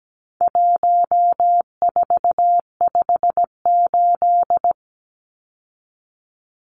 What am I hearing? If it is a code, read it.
Morse code, heard as a keyed tone, about 17 wpm, 700 Hz. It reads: 1458
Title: Numbers - 4-Digit 17wpm